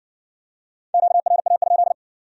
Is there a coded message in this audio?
Yes: Morse code from 0.9 to 1.9 s, heard as a keyed tone, about 56 wpm, 700 Hz. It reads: QRA?